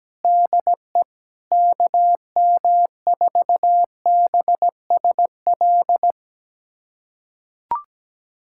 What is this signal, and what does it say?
Morse code, heard as a keyed tone, about 17 wpm, 700 Hz. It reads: DE KM4BSL
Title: USA Callsign DE Callsign - 6 Characters - 17wpm